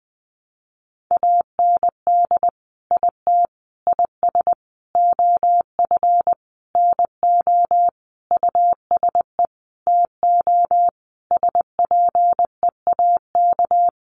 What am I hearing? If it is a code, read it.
Morse code, heard as a keyed tone, about 19 wpm, 700 Hz. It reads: AND IT IS OF NO USE TO SPEAK
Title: Sentences from Top 1100 Words 20wpm